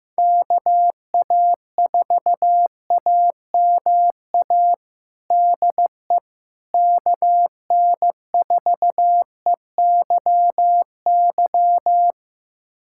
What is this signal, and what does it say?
Morse code, heard as a keyed tone, about 15 wpm, 700 Hz. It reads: KA4AMA DE KN4EYY
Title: USA Callsign DE Callsign 15wpm